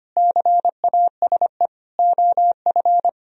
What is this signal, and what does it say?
Morse code, heard as a keyed tone, about 30 wpm, 700 Hz. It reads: CASE OF
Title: Speed-Racing - Sets of 5 Words 25wpm